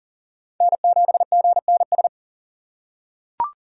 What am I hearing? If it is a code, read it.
Morse code, heard as a keyed tone, about 39 wpm, 700 Hz. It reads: N7GNS
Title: USA Call Signs - 4x Character Spacing - 40wpm